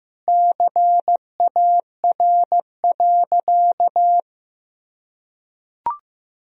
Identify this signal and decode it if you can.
Morse code, heard as a keyed tone, about 16 wpm, 700 Hz. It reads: CAR.
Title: Sentences from Top 1000 Words - Review Part 2 - 15wpm